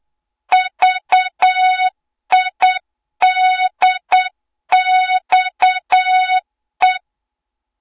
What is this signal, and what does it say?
Morse code, heard as a keyed tone, about 8 wpm, 3700 Hz, with noise. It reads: VIDXE
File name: Morze-HA2MN / hang / 20 GYAKORLAT